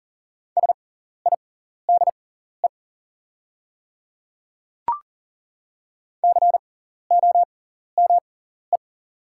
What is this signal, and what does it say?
Morse code, heard as a keyed tone, about 40 wpm, spaced as for 14 wpm, 700 Hz. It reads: SIDE COME
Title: 6x Character Spacing - Top 100 Words - 40wpm